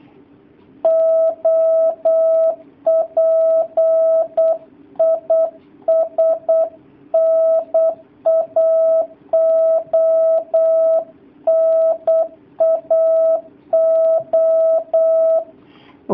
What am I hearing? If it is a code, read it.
Morse code, heard as a keyed tone, about 8 wpm, 650 Hz. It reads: OP IS NAO NAO
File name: jh8cbh / 2019cw / sp08- w